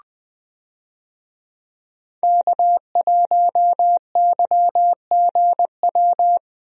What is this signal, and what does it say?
Morse code, heard as a keyed tone, about 20 wpm, 700 Hz. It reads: K1YGW
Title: USA Call Signs - 5 Characters - Repeated 3x - 20wpm